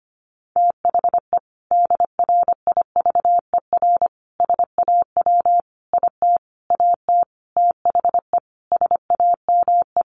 Morse code, heard as a keyed tone, about 25 wpm, 700 Hz. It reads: THE DRIVER SAW IT AT THE SAME
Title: Sentences from Top 1000 Words 25wpm